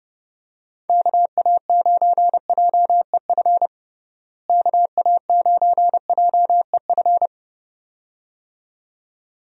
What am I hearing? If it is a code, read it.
Morse code, heard as a keyed tone, about 30 wpm, 700 Hz. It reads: KA9JEF KA9JEF
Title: USA Call Signs - 6 Characters - Repeated 2x - 30wpm